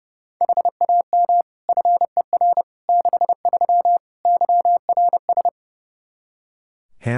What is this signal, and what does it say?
Morse code, heard as a keyed tone, about 30 wpm, 700 Hz. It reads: HAM FER 63 YRS
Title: Practice QSO - Exchange 3 - More Info - 30wpm